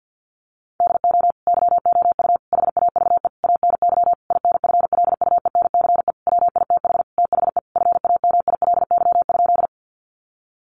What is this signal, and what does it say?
Morse code, heard as a keyed tone, about 49 wpm, 700 Hz. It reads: DO YOU HAVE ANY INFLUENCE WITH THE FAMILY?
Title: Sentences from Top 1100 Words 50wpm